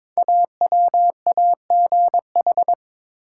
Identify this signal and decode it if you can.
Morse code, heard as a keyed tone, about 22 wpm, 700 Hz. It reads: AWAGH
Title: Code Groups - 5 Letter:Numbers - 22wpm